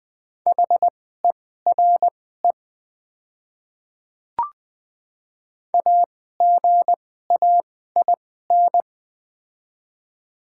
Morse code, heard as a keyed tone, about 20 wpm, spaced as for 14 wpm, 700 Hz. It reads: HERE AGAIN
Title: 2x Character Spacing - Top 200 Words - 20wpm